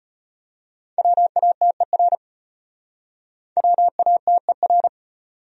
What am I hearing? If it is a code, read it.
Morse code, heard as a keyed tone, about 33 wpm, 700 Hz. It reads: WATER WATER